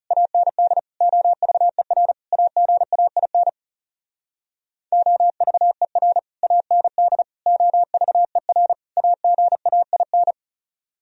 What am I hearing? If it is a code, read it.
Morse code, heard as a keyed tone, about 40 wpm, 700 Hz. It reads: AND OVER AGAIN OVER AND OVER AGAIN